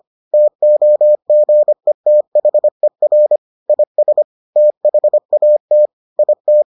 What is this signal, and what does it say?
Morse code, heard as a keyed tone, about 25 wpm, 600 Hz. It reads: TOGETHER IS THAT IT